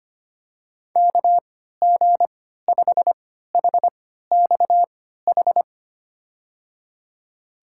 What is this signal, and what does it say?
Morse code, heard as a keyed tone, about 25 wpm, spaced as for 14 wpm, 700 Hz. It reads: KG5HXH
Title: USA Call Signs - 6 Characters - 3x Character Spacing - 25wpm